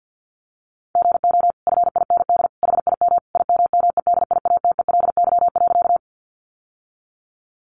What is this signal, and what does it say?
Morse code, heard as a keyed tone, about 50 wpm, 700 Hz. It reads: GO FIND HIM IMMEDIATELY.
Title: Sentences from Top 1100 Words 50wpm